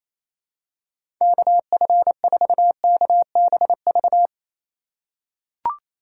Morse code, heard as a keyed tone, about 28 wpm, 700 Hz. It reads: KF4KBV
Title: USA Call Signs - 6 Characters - 2x Character Spacing - 28wpm